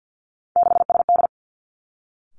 Morse code, heard as a keyed tone, about 50 wpm, 700 Hz. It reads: 6SD
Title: Three Challenging Letter-Number Combinations 50wpm